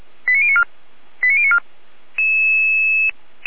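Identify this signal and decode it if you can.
Five-tone selective call (ZVEI), heard as quick runs of five tones, tones of about 70 ms. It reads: 89094 89094
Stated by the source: ZVEI five-tone selective call received on SDR.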